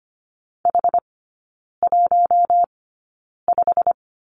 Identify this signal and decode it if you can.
Morse code, heard as a keyed tone, about 25 wpm, spaced as for 9 wpm, 700 Hz. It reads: H15
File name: Lesson 15e - Number 5 - Three Characters 25wpm